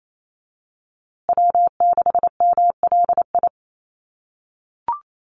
Morse code, heard as a keyed tone, about 28 wpm, 700 Hz. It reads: W6MLI
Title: USA Call Signs Repeated 2x 28wpm